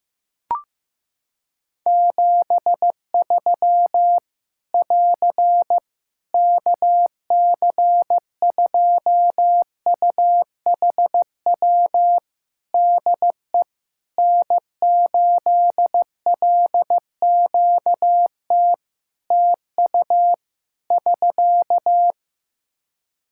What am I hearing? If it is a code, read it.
Morse code, heard as a keyed tone, about 15 wpm, 700 Hz. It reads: 73 <AR> KC2UHW DE N8LQT TU <SK>
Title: Practice QSO - Exchange 4 - Ending - 15wpm